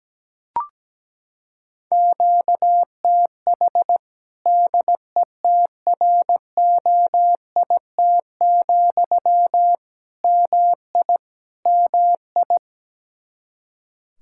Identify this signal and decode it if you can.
Morse code, heard as a keyed tone, about 17 wpm, 700 Hz. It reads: QTH DETROIT, MI MI
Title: Practice QSO - Exchange 1 - Beginning - 17wpm